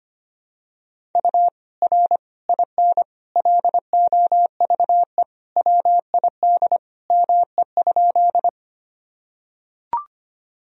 Morse code, heard as a keyed tone, about 25 wpm, 700 Hz. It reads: U R IN LOVE WID ME?
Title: Sentences from Top 400 Words and Abbreviations 25wpm